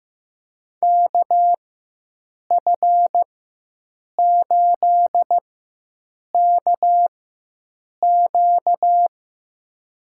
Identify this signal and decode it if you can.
Morse code, heard as a keyed tone, about 15 wpm, spaced as for 7 wpm, 700 Hz. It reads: KF8KQ
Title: USA Call Signs - 5 Characters - 4x Character Spacing - 15wpm